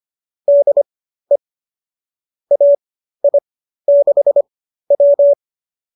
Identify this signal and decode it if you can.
Morse code, heard as a keyed tone, about 25 wpm, spaced as for 13 wpm, 570 Hz. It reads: DE AI6W
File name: qso207 13